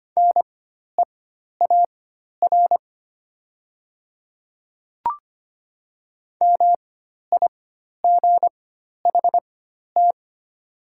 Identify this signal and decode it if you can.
Morse code, heard as a keyed tone, about 25 wpm, spaced as for 12 wpm, 700 Hz. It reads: NEAR MIGHT
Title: 4x Character Spacing - Top 500 Words - 25wpm